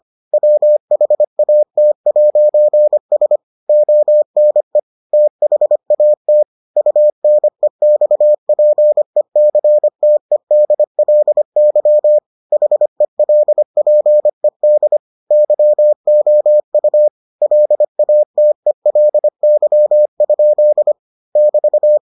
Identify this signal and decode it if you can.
Morse code, heard as a keyed tone, about 25 wpm, 600 Hz. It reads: WHAT'S ONE THAT UNEXPECTEDLY HELPED YOU LATELY? <BT>